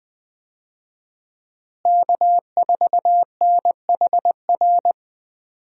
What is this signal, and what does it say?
Morse code, heard as a keyed tone, about 20 wpm, 700 Hz. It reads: K4NHR
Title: USA Call Signs - 5 Characters - Repeated 3x - 20wpm